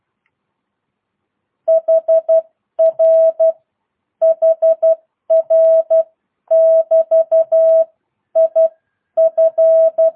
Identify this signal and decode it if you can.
Morse code, heard as a keyed tone, about 12 wpm, spaced as for 10 wpm, 650 Hz. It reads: HR HR <BT> IF